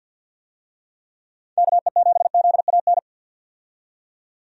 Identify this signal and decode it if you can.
Morse code, heard as a keyed tone, about 46 wpm, 700 Hz. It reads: KE7ZAN
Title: USA Call Signs - 6 Characters - 50wpm